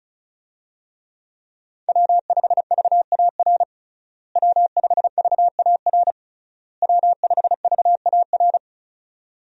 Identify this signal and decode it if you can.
Morse code, heard as a keyed tone, about 36 wpm, 700 Hz. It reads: W5VAR W5VAR W5VAR
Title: USA Call Signs - 5 Characters - Repeated 3x - 35wpm